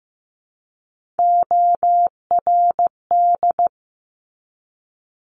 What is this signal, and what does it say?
Morse code, heard as a keyed tone, about 15 wpm, 700 Hz. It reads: ORD
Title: Top Three Letter Combinations in Words 15wpm